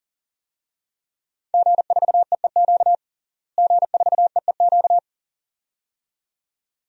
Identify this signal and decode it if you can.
Morse code, heard as a keyed tone, about 40 wpm, 700 Hz. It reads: G4EEQ G4EEQ
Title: DXCC-100 Call Signs - 5 Characters - Repeated 2x - 40wpm